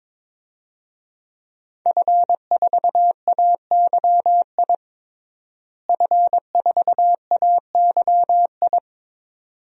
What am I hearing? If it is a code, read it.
Morse code, heard as a keyed tone, about 22 wpm, 700 Hz. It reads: F4AYI F4AYI
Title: DXCC-100 Call Signs - 5 Characters - Repeated 2x - 22wpm